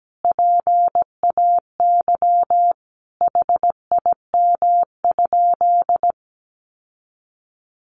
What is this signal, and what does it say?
Morse code, heard as a keyed tone, about 17 wpm, 700 Hz. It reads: PAY HIM?
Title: Sentences from Top 700 Words 17wpm